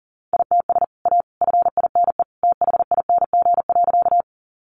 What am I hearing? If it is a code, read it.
Morse code, heard as a keyed tone, about 40 wpm, 700 Hz. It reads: ITS A FINE THING.
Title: Sentences from Top 600 Words 40wpm